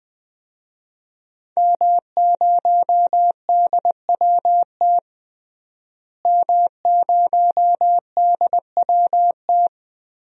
Morse code, heard as a keyed tone, about 20 wpm, 700 Hz. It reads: M0DWT M0DWT
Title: DXCC-100 Call Signs - 5 Characters - Repeated 3x - 20wpm